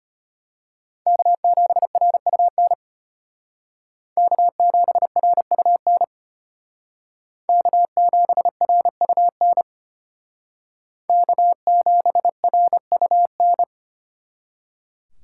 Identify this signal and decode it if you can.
Morse code, heard as a keyed tone, about 32 wpm, 700 Hz. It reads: K7RUN K7RUN K7RUN K7RUN